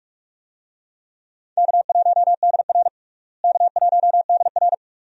Morse code, heard as a keyed tone, about 46 wpm, 700 Hz. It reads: K1DR K1DR
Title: USA Call Signs - 4 Characters - Repeated 2x - 45wpm